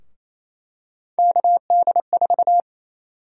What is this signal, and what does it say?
Morse code, heard as a keyed tone, about 28 wpm, 700 Hz. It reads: KD4
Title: USA Call Sign Prefixes - 28wpm